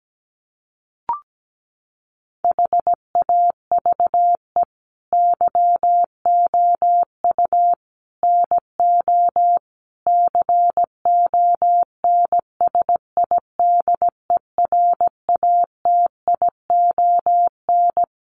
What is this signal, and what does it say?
Morse code, heard as a keyed tone, about 17 wpm, 700 Hz. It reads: HAVE YOU NO CONSIDERATION
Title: Sentences from Top 1500 Words 17wpm